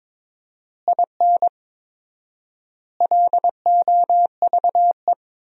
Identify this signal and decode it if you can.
Morse code, heard as a keyed tone, about 22 wpm, 700 Hz. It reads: IN LOVE
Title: 4x Word-Spacing - Sentences from Top 400 Words - 22wpm